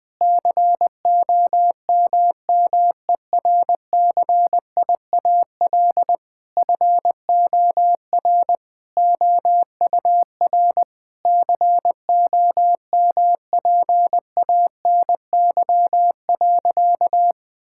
Morse code, heard as a keyed tone, about 20 wpm, 700 Hz. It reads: COMMERCIAL FOR OUR COMPANY.
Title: Sentences from Top 1000 Words - Review Part 5 - 20wpm